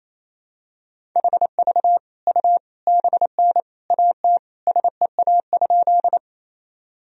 Morse code, heard as a keyed tone, about 28 wpm, 700 Hz. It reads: HV U BN AT SEA?
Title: Sentences from Top 300 Words and Abbreviations 28wpm